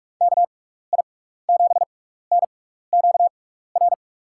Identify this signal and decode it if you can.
Morse code, heard as a keyed tone, about 46 wpm, spaced as for 15 wpm, 700 Hz. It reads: KI7NQR
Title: USA Call Signs - 6 Characters - 6x Character Spacing - 45wpm